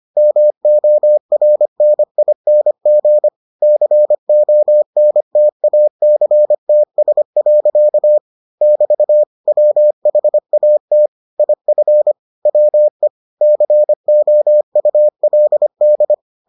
Morse code, heard as a keyed tone, about 25 wpm, 600 Hz. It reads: MORNING CONTACTS. <BT> WHAT IF WE COULD